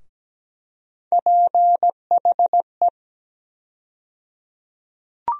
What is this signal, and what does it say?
Morse code, heard as a keyed tone, about 17 wpm, 700 Hz. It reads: PHE
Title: USA Call Sign Postfixes - 3 Letters 17wpm